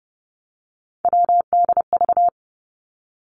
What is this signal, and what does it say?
Morse code, heard as a keyed tone, about 30 wpm, 700 Hz. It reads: WDV
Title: Three Challenging Letter-Number Combinations 30wpm